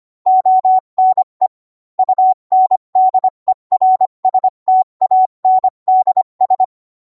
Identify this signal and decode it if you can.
Morse code, heard as a keyed tone, about 25 wpm, 750 Hz. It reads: ONE UNDERSTANDS